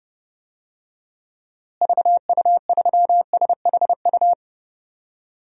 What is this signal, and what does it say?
Morse code, heard as a keyed tone, about 30 wpm, 700 Hz. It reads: VU3SHU
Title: DXCC-100 Call Signs - 6 Characters - Repeated 2x - 30wpm